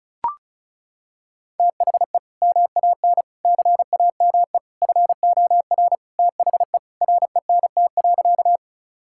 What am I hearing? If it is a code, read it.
Morse code, heard as a keyed tone, about 34 wpm, 700 Hz. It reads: THE MAN CAME FOR THE RENT.
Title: Sentences from Top 1000 Words - Review Part 5 - 35wpm